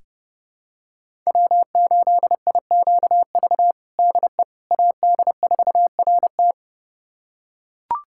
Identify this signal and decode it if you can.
Morse code, heard as a keyed tone, about 30 wpm, 700 Hz. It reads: W8IQV DE AD4RT
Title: USA Callsign DE Callsign 30wpm